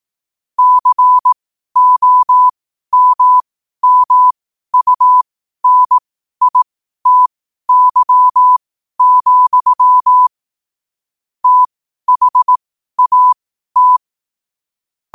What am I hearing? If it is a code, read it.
Morse code, heard as a keyed tone, about 18 wpm, spaced as for 12 wpm, 1000 Hz. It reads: COMMUNITY, THAT
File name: qotdfiles / 12wpm